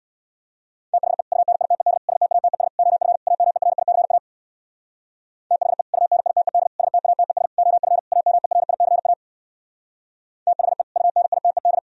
Morse code, heard as a keyed tone, about 61 wpm, 700 Hz. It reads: THE UNITED STATES OF AMERICA THE UNITED STATES OF AMERICA THE UNITED
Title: Speed-Racing - Sets of 5 Words 50wpm